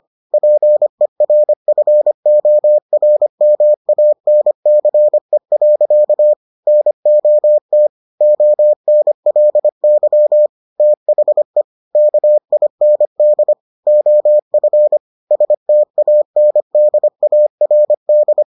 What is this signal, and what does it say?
Morse code, heard as a keyed tone, about 25 wpm, 600 Hz. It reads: PERFORMANCE. NOT ONLY THE KIND OF STANDARD